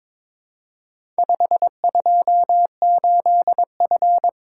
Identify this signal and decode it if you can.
Morse code, heard as a keyed tone, about 22 wpm, 700 Hz. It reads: 528F
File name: practice / 22wpm / Four Letter-Number Combinations - 22wpm